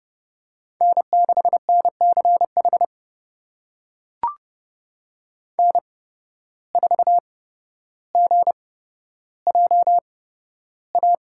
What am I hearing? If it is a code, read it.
Morse code, heard as a keyed tone, about 30 wpm, spaced as for 13 wpm, 700 Hz. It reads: N6NCH N4GJA
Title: USA Call Signs - 5 Characters - 8x Character Spacing - 30wpm